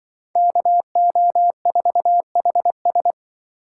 Morse code, heard as a keyed tone, about 23 wpm, 700 Hz. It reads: KO4HS
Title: Speed Racing - USA Call Signs 20wpm